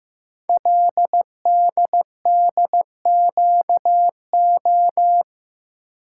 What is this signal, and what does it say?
Morse code, heard as a keyed tone, about 15 wpm, 690 Hz. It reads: LDDQO
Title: Code Groups - 5 Letters - 15wpm